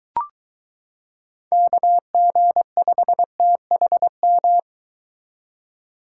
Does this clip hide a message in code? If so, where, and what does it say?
Yes: Morse code from 1.5 to 4.6 s, heard as a keyed tone, about 22 wpm, 700 Hz. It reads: KG5THM